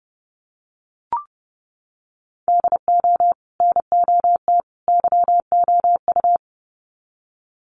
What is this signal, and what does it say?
Morse code, heard as a keyed tone, about 30 wpm, 700 Hz. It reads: DO NOT YOU
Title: Sets of 3 Words 30wpm